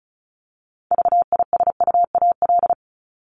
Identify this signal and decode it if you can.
Morse code, heard as a keyed tone, about 34 wpm, 700 Hz. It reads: VISUAL
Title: Top 751-1000 Words 35wpm